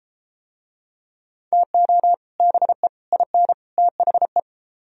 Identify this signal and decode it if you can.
Morse code, heard as a keyed tone, about 31 wpm, 700 Hz. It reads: TO BE IN THE